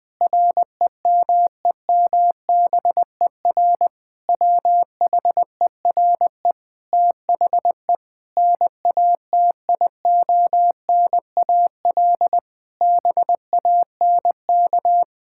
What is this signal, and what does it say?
Morse code, heard as a keyed tone, about 20 wpm, 700 Hz. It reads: REMEMBER WHERE THE NATIONAL BANK
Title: Sentences from Top 1000 Words - Review Part 6 - 20wpm